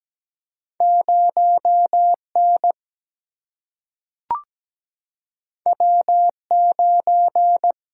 Morse code, heard as a keyed tone, about 17 wpm, 700 Hz. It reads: 0N W9
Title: Single-Letter Single-Number 17wpm